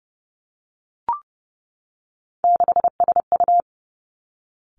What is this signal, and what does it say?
Morse code, heard as a keyed tone, about 31 wpm, 700 Hz. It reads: 6SU
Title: Three Challenging Letter-Number Combinations 30wpm